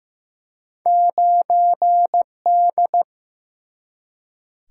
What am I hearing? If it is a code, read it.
Morse code, heard as a keyed tone, about 15 wpm, 700 Hz. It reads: 9D
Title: Single-Letter Single-Number 15wpm